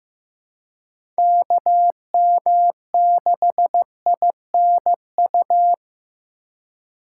Morse code, heard as a keyed tone, about 15 wpm, 700 Hz. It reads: KM6INU
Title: USA Call Signs - 6 Characters - 2x Character Spacing - 15wpm